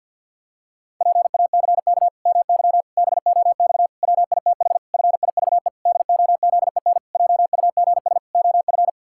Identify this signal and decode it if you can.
Morse code, heard as a keyed tone, about 50 wpm, 700 Hz. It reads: PACK MY BOX WITH FIVE DOZEN JUGS OF
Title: Pangram Challenge v2 - 50wpm